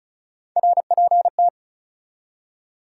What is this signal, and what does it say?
Morse code, heard as a keyed tone, about 34 wpm, 700 Hz. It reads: RPT
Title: Top 180 QSO Elements 35wpm